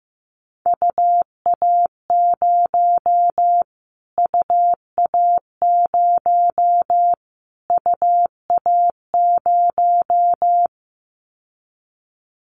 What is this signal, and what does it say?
Morse code, heard as a keyed tone, about 15 wpm, 700 Hz. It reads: UA0 UA0 UA0
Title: Callsign Prefixes for Top 25 DXCC - Repeated 3x - 15wpm